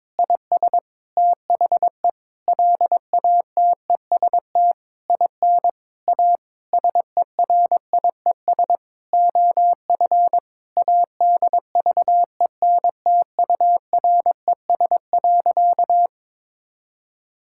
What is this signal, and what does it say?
Morse code, heard as a keyed tone, about 22 wpm, 700 Hz. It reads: IS THE LATEST IN A SERIES OF ADVENTURES.
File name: Sentences with 7-Word Phrases - 22wpm